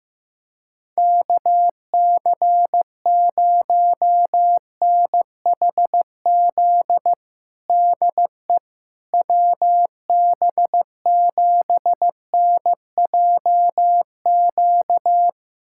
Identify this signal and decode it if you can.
Morse code, heard as a keyed tone, about 15 wpm, 700 Hz. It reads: KC0NHZ DE WB7NJQ
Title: USA Callsign DE Callsign - 6 Characters - 15wpm